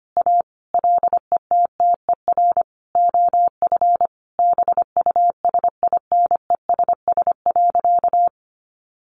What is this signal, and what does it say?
Morse code, heard as a keyed tone, about 25 wpm, 700 Hz. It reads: A LETTER OF BUSINESS.
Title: Sentences from Top 1400 Words 25wpm